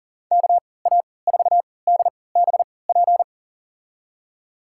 Morse code, heard as a keyed tone, about 39 wpm, spaced as for 22 wpm, 700 Hz. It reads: KA4DBP
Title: USA Call Signs - 6 Characters - 3x Character Spacing - 40wpm